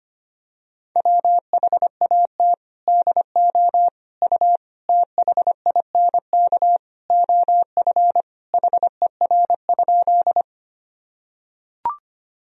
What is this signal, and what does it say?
Morse code, heard as a keyed tone, about 25 wpm, 700 Hz. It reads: WHAT DO U THINK OF HER?
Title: Sentences from Top 200 Words and Abbreviations 25wpm